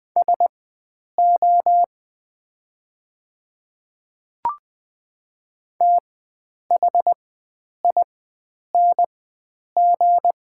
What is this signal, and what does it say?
Morse code, heard as a keyed tone, about 20 wpm, spaced as for 9 wpm, 690 Hz. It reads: SO THING
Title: 4x Character Spacing - Top 100 Words - 20wpm